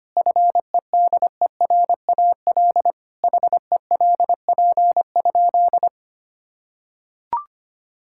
Morse code, heard as a keyed tone, about 25 wpm, 700 Hz. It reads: FEDERAL HELP?
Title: Sentences from Top 1000 Words - Review Part 5 - 25wpm